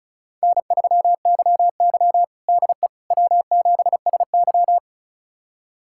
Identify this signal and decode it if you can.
Morse code, heard as a keyed tone, about 36 wpm, 700 Hz. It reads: N3YY DE W7SY
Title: USA Callsign DE Callsign - 4 Characters - 35wpm